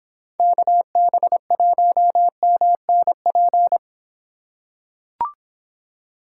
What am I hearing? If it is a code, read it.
Morse code, heard as a keyed tone, about 24 wpm, 700 Hz. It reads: KB1MNP